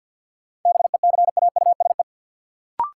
Morse code, heard as a keyed tone, about 53 wpm, 700 Hz. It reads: BECAUSE
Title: Sets of 2 Words - New Word Drill - 50wpm